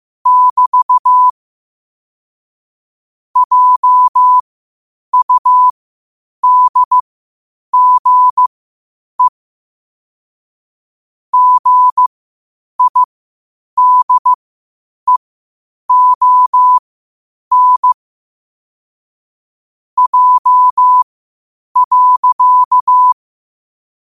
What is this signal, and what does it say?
Morse code, heard as a keyed tone, about 15 wpm, spaced as for 8 wpm, 1000 Hz. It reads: <BT> JUDGE GIDEON J.
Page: New quotes every day in morse code at 8 Words per minute.